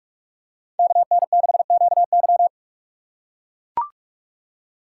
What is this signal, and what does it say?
Morse code, heard as a keyed tone, about 44 wpm, 700 Hz. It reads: KN6QY